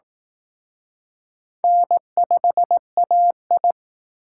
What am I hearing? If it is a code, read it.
Morse code, heard as a keyed tone, about 19 wpm, 700 Hz. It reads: N5AI
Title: Speed Racing - USA 1x2 Call Signs 15wpm